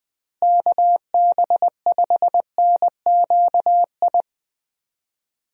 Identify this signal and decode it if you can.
Morse code, heard as a keyed tone, about 20 wpm, 700 Hz. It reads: KB5NQI
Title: USA Call Signs - 6 Characters - Repeated 2x - 20wpm